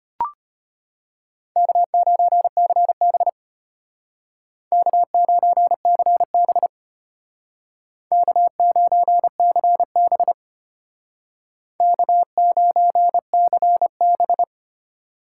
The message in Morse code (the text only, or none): K9CB K9CB K9CB K9CB